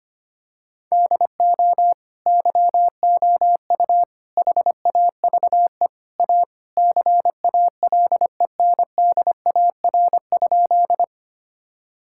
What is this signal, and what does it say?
Morse code, heard as a keyed tone, about 25 wpm, 700 Hz. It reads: DO YOU HAVE A CALENDAR?
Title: Sentences from Top 1600 Words 25wpm